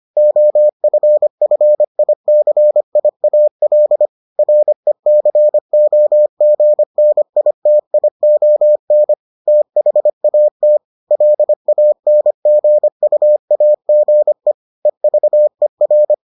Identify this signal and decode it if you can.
Morse code, heard as a keyed tone, about 25 wpm, 600 Hz. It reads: OFFICIAL RECOGNITION THAT LANGUAGE EVER